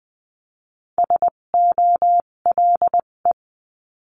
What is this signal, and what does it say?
Morse code, heard as a keyed tone, about 20 wpm, spaced as for 17 wpm, 700 Hz. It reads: SOLE
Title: 257 Lesson 42b - Words 14-20wpm